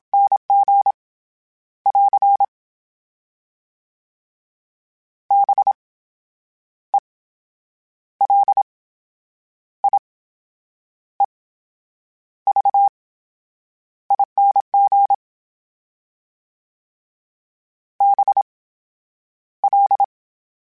Morse code, heard as a keyed tone, about 27 wpm, spaced as for 8 wpm, 800 Hz. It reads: NG <AR> BELIEVING BL
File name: Special INGs / Special INGs-mp3s / VerySlowToFaster Ing @25